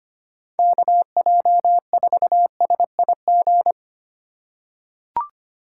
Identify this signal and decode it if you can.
Morse code, heard as a keyed tone, about 25 wpm, 700 Hz. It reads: KJ4SIG
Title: USA Call Signs - 6 Characters - 3x Character Spacing - 25wpm